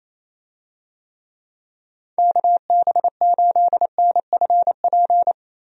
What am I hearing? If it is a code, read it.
Morse code, heard as a keyed tone, about 28 wpm, 700 Hz. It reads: KB8NFP
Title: USA Call Signs - 6 Characters - Repeated 2x - 28wpm